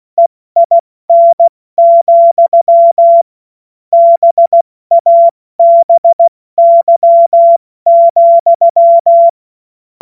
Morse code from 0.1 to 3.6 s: EIN,